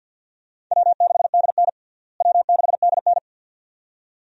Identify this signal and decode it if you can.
Morse code, heard as a keyed tone, about 46 wpm, 700 Hz. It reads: W6DN W6DN
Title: USA Call Signs - 4 Characters - Repeated 2x - 50wpm